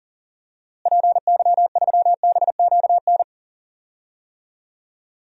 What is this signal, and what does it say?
Morse code, heard as a keyed tone, about 37 wpm, 700 Hz. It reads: PY3BQN
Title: DXCC-100 Call Signs - 6 Characters - Repeated 2x - 40wpm